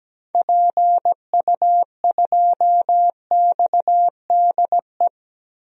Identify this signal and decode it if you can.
Morse code, heard as a keyed tone, about 17 wpm, 700 Hz. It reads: PU2XDE
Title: DXCC-100 Call Signs - 6 Characters - 17wpm